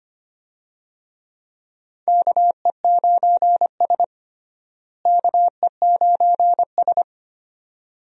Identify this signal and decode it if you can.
Morse code, heard as a keyed tone, about 25 wpm, 700 Hz. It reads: KE9S KE9S
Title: USA Call Signs - 4 Characters - Repeated 2x - 25wpm